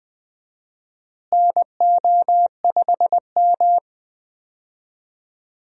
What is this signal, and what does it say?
Morse code, heard as a keyed tone, about 20 wpm, 700 Hz. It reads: NO5M
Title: USA Call Signs - 4 Characters - Repeated 3x - 20wpm